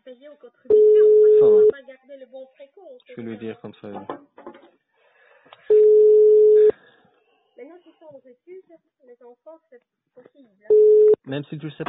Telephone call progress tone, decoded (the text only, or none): ringback tone